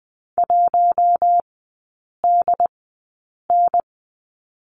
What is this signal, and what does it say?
Morse code, heard as a keyed tone, about 20 wpm, spaced as for 8 wpm, 700 Hz. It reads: 1DN
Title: 067 Lesson 12e - Letter d - Three Characters 20wpm